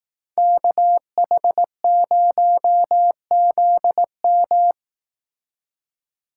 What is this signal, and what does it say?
Morse code, heard as a keyed tone, about 18 wpm, 700 Hz. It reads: KH0ZM
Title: Speed Racing - USA 2x2 Call Signs 15wpm